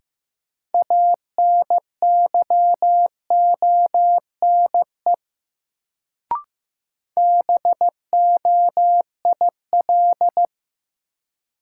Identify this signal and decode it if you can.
Morse code, heard as a keyed tone, about 15 wpm, 700 Hz. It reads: ANYONE BOIL
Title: Sets of 5 Words Encore - New Word Drill - 15wpm